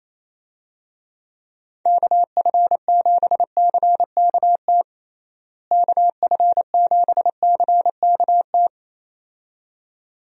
Morse code, heard as a keyed tone, about 27 wpm, 700 Hz. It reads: KF7CKT KF7CKT
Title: USA Call Signs - 6 Characters - Repeated 2x - 28wpm